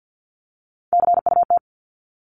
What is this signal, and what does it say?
Morse code, heard as a keyed tone, about 50 wpm, 700 Hz. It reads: CUT
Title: Top 500 Words Repeated 50wpm